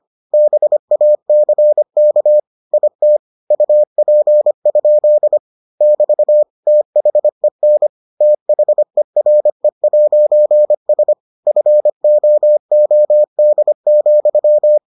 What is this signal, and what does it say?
Morse code, heard as a keyed tone, about 25 wpm, 600 Hz. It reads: BACK IT UP? <BT> THEN THERE'S FOOD,